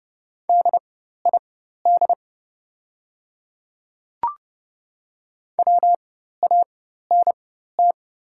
Morse code, heard as a keyed tone, about 30 wpm, spaced as for 14 wpm, 700 Hz. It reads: DID WANT
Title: 4x Character Spacing - Top 200 Words - 30wpm